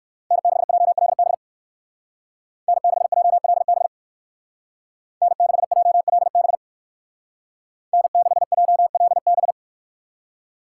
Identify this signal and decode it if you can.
Morse code, heard as a keyed tone, about 64 wpm, 700 Hz. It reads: N6JLB N6JLB N6JLB N6JLB